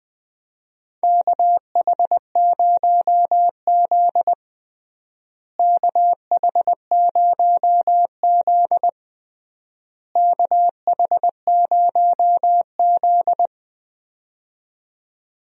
Morse code, heard as a keyed tone, about 20 wpm, 700 Hz. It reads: KH0Z KH0Z KH0Z
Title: USA Call Signs - 4 Characters - Repeated 3x - 20wpm